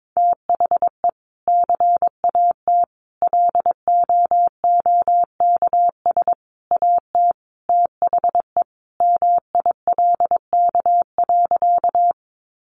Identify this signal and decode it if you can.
Morse code, heard as a keyed tone, about 22 wpm, 700 Hz. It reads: THE CAT LOOKS AT THE MILK.
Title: Additional Easy Phrases 22wpm